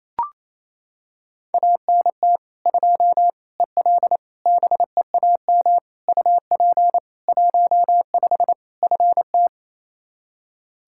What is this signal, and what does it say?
Morse code, heard as a keyed tone, about 27 wpm, 700 Hz. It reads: ANT 2 EL BEAM UP 15 FT
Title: Practice QSO - Exchange 2 - Info - 28wpm